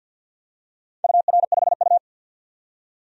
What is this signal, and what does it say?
Morse code, heard as a keyed tone, about 52 wpm, 700 Hz. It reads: UR5U
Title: DXCC-100 Call Signs - 4 Characters - 50wpm